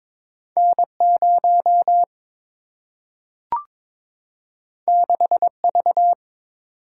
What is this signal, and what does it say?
Morse code, heard as a keyed tone, about 22 wpm, 700 Hz. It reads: N 0 6 V
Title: Single-Letter Single-Number 22wpm